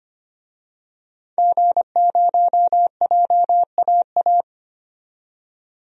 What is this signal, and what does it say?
Morse code, heard as a keyed tone, about 25 wpm, 700 Hz. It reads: G0JAA
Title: DXCC-100 Call Signs - 5 Characters - 25wpm